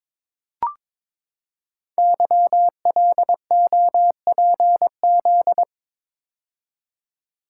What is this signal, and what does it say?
Morse code, heard as a keyed tone, about 23 wpm, 700 Hz. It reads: YLOPZ
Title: Code Groups - 5 Letters - 22wpm